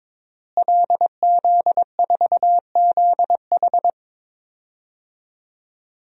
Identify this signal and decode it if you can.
Morse code, heard as a keyed tone, about 22 wpm, 700 Hz. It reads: LZ4ZH
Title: DXCC-100 Call Signs - 5 Characters - Repeated 3x - 22wpm